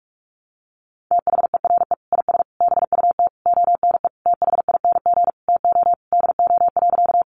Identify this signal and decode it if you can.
Morse code, heard as a keyed tone, about 44 wpm, 700 Hz. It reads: THERE IS BUT ONE THING TO DO.
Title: Sentences from Top 100 Words 45wpm